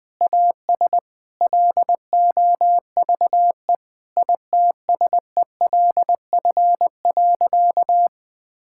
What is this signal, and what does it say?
Morse code, heard as a keyed tone, about 20 wpm, 700 Hz. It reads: AS LOVE ITSELF.
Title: Sentences from Top 1000 Words - Review Part 5 - 20wpm